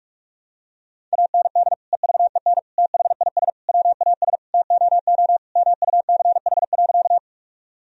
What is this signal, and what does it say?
Morse code, heard as a keyed tone, about 46 wpm, 700 Hz. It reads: AND EVEN THIS WAS TOO MUCH.
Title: Sentences from Top 1000 Words - Review Part 1 - 45wpm